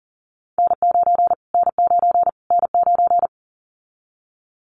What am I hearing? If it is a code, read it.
Morse code, heard as a keyed tone, about 40 wpm, 700 Hz. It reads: N9 N9 N9
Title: Callsign Prefixes for Top 25 DXCC - Repeated 3x - 40wpm